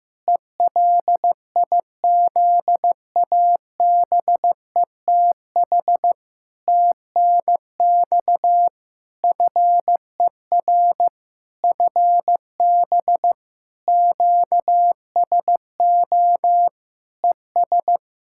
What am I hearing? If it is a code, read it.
Morse code, heard as a keyed tone, about 15 wpm, 700 Hz. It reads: ELIZABETH TNX FER FB QSO ES
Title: Practice QSO - Exchange 4 - Ending - 15wpm